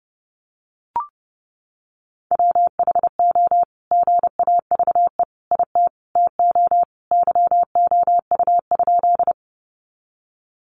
Morse code, heard as a keyed tone, about 30 wpm, 700 Hz. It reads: WHO GAVE IT TO YOU?
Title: Sentences from Top 500 Words 30wpm